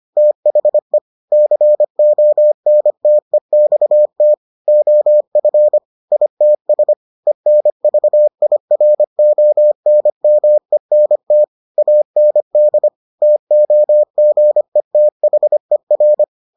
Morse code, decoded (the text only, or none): THE CONTEXT OF ITS ENVIRONMENT AND TOGETHER